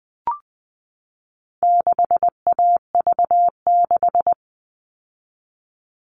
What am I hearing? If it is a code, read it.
Morse code, heard as a keyed tone, about 20 wpm, 700 Hz. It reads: 6AV6
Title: All Cut Numbers - 4-Digit 20wpm